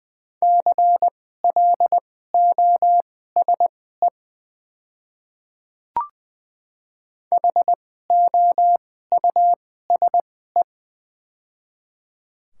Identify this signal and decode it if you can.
Morse code, heard as a keyed tone, about 20 wpm, spaced as for 14 wpm, 690 Hz. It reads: CLOSE HOUSE
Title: 2x Character Spacing - Top 300 Words - 20wpm